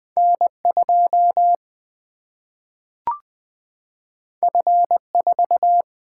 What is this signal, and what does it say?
Morse code, heard as a keyed tone, about 20 wpm, 700 Hz. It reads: N 2 F 4
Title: Single-Letter Single-Number 20wpm